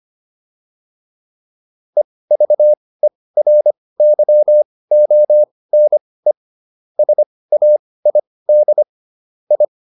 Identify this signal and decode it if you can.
Morse code, heard as a keyed tone, about 25 wpm, spaced as for 18 wpm, 600 Hz. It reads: EVERYONE SAID I